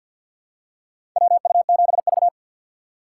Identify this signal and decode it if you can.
Morse code, heard as a keyed tone, about 53 wpm, 700 Hz. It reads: WU7V
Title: USA Call Signs - 4 Characters - 3x Character Spacing - 50wpm